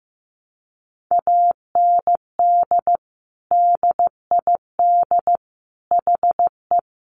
Morse code, heard as a keyed tone, about 14 wpm, 700 Hz. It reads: AND DID HE
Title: Sentences from Top 1100 Words 15wpm